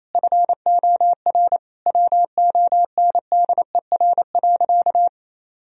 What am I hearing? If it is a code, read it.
Morse code, heard as a keyed tone, about 28 wpm, 700 Hz. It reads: FOR WONDER.
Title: Sentences from Top 1000 Words - Review Part 3 - 28wpm